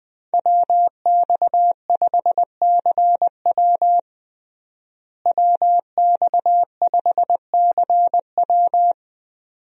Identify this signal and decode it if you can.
Morse code, heard as a keyed tone, about 21 wpm, 700 Hz. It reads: WX5CW WX5CW
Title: USA Call Signs - 5 Characters - Repeated 2x - 20wpm